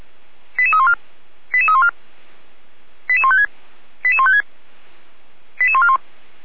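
Five-tone selective call (ZVEI), heard as quick runs of five tones, tones of about 70 ms. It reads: 88314 88314 88156 88156 88141
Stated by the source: Five-tone ZVEI selective call 88133, received on an SDR; from the feuerwehr-orlamuende.org fire brigade audio server.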